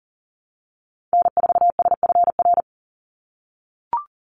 Morse code, heard as a keyed tone, about 40 wpm, 700 Hz. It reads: N4SFR
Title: NA SOTA Activator Callsigns 40wpm